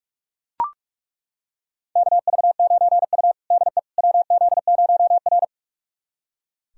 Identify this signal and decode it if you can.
Morse code, heard as a keyed tone, about 42 wpm, 700 Hz. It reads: KV9U DE WZ0R
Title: USA Callsign DE Callsign - 4 Characters - 45wpm